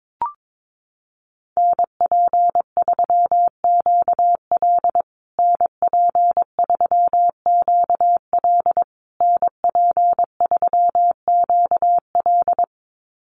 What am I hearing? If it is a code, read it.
Morse code, heard as a keyed tone, about 22 wpm, 700 Hz. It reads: NP3QL NP3QL NP3QL
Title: USA Call Signs Repeated 3x 22wpm